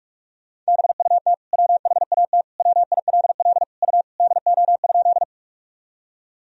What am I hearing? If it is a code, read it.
Morse code, heard as a keyed tone, about 46 wpm, 700 Hz. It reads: BUT WHAT WILL U DO?
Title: Sentences from Top 100 Words and Abbreviations 45wpm